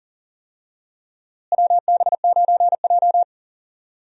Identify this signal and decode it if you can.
Morse code, heard as a keyed tone, about 40 wpm, 700 Hz. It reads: WB9J
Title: USA Call Signs - 4 Characters - Repeated 3x - 40wpm